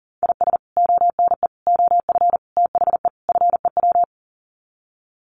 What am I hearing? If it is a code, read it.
Morse code, heard as a keyed tone, about 40 wpm, 700 Hz. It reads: IS ONE OF THE FEW
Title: Sets of 5 Words 40wpm